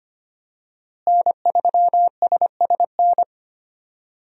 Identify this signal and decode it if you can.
Morse code, heard as a keyed tone, about 25 wpm, 700 Hz. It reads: N3SSN
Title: USA Call Signs - 5 Characters - Repeated 3x - 25wpm